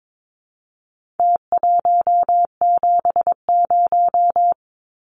Morse code, heard as a keyed tone, about 22 wpm, 700 Hz. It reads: T170
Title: Numbers - Contest Style - 4-Digit 22wpm